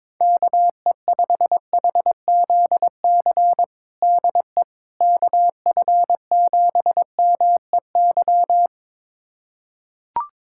Morse code, heard as a keyed tone, about 21 wpm, 700 Hz. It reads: KE5HZC DE KF7MEY
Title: USA Callsign DE Callsign - 6 Characters - 22wpm